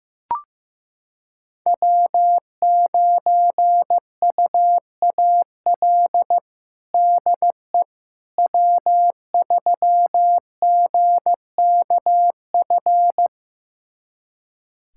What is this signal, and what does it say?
Morse code, heard as a keyed tone, about 16 wpm, 700 Hz. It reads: W9UAL DE W3GKF
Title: USA Callsign DE Callsign - 5 Characters - 15wpm